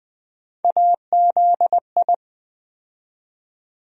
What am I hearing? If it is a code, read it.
Morse code, heard as a keyed tone, about 20 wpm, 700 Hz. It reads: AZI
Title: USA Call Sign Postfixes - 3 Letters 20wpm